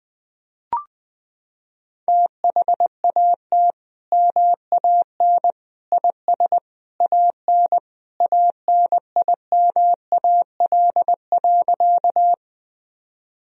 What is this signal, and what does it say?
Morse code, heard as a keyed tone, about 20 wpm, 700 Hz. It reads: THAT MAN IS AN ANIMAL.
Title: Sentences from Top 1000 Words - Review Part 1 - 20wpm